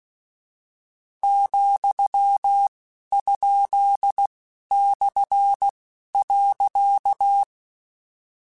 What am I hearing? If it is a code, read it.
Morse code, heard as a keyed tone, about 16 wpm, spaced as for 11 wpm, 800 Hz. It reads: ,?/.
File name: Misc / Misc-mp3s / Punct @10